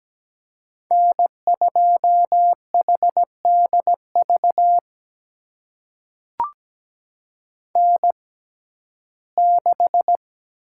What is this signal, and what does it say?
Morse code, heard as a keyed tone, about 17 wpm, spaced as for 12 wpm, 700 Hz. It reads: N2HDV N6
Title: USA Call Signs - 5 Characters - 6x Character Spacing - 17wpm